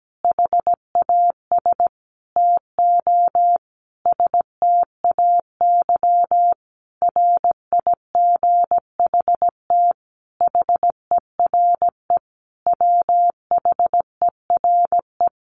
Morse code, heard as a keyed tone, about 17 wpm, 700 Hz. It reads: HAS TO STAY RIGHT HERE WHERE
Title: A Sound Of Thunder 17wpm